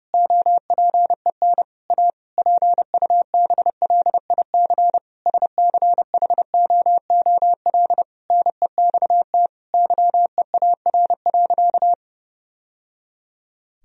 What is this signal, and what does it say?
Morse code, heard as a keyed tone, about 30 wpm, 700 Hz. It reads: OPEN A PUBLIC SCHOOL NEXT YEAR.
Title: Sentences from Top 1000 Words - Review Part 3 - 30wpm